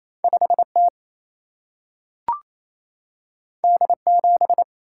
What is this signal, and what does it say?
Morse code, heard as a keyed tone, about 28 wpm, 700 Hz. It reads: 5T D7
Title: Single-Letter Single-Number 28wpm